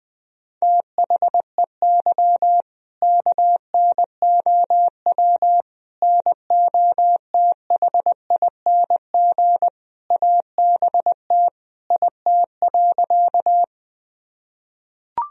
Morse code, heard as a keyed tone, about 21 wpm, 700 Hz. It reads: THEY KNOW NOTHING ABT IT.
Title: Sentences from Top 500 Words and Abbreviations 20wpm